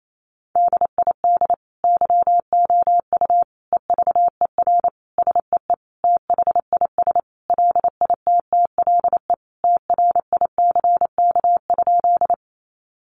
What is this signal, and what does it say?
Morse code, heard as a keyed tone, about 28 wpm, 700 Hz. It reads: DID YOU EVER SEE THIS LITTLE TRICK?
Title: Sentences from Top 1500 Words 28wpm